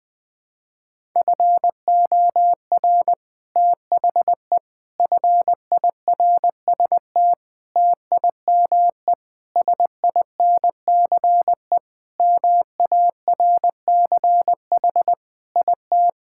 Morse code, heard as a keyed tone, about 21 wpm, 700 Hz. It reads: FOR THE FIRST TIME SINCE MARCH IT
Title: Sentences with 5-Word Phrases - 20wpm